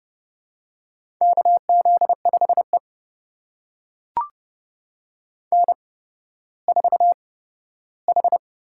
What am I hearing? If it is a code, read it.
Morse code, heard as a keyed tone, about 30 wpm, spaced as for 15 wpm, 700 Hz. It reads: KZ5E N4H
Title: USA Call Signs - 4 Characters - 8x Character Spacing - 30wpm